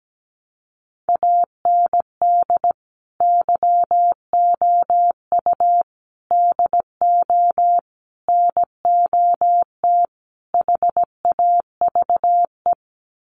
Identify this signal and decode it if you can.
Morse code, heard as a keyed tone, about 17 wpm, 700 Hz. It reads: AND YOU DO NOT HAVE
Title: Sets of 5 Words Encore 17wpm